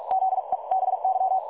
Morse code, heard as a keyed tone, about 46 wpm, spaced as for 37 wpm, 760 Hz. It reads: ZEBY
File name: dlazaawansowanych / nagraniazpasma / websdr recording 3565.0kHz